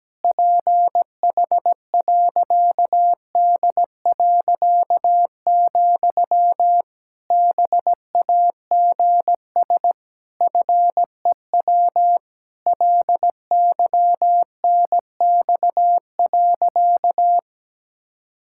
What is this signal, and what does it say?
Morse code, heard as a keyed tone, about 17 wpm, 700 Hz. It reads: PH.D., BAGS FEW LYNX.
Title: Pangram Challenge v2 - 17wpm